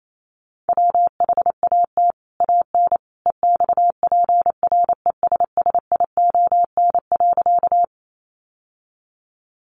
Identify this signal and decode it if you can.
Morse code, heard as a keyed tone, about 28 wpm, 700 Hz. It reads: WHAT AN EXPRESSION.
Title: Sentences from Top 1200 Words 28wpm